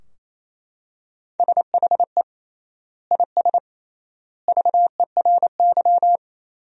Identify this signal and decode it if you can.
Morse code, heard as a keyed tone, about 28 wpm, 700 Hz. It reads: SHE IS VERY
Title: 3x Word-Spacing - Sentences from Top 400 Words - 28wpm